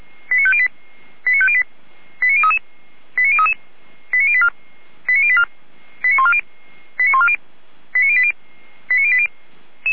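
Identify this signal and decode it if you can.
Five-tone selective call (ZVEI), heard as quick runs of five tones, tones of about 70 ms. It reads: 89508 89508 89033 89033 89084 89084 89140 89140 89080 89080
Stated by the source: ZVEI five-tone selective call received on SDR.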